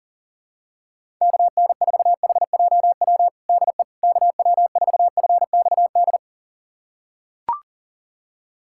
Morse code, heard as a keyed tone, about 40 wpm, 700 Hz. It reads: KN4HJW DE KW4FXD
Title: USA Callsign DE Callsign - 6 Characters - 40wpm